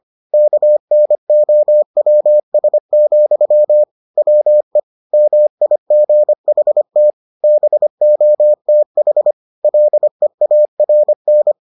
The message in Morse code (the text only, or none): KNOWS, WE MIGHT BOTH LEARN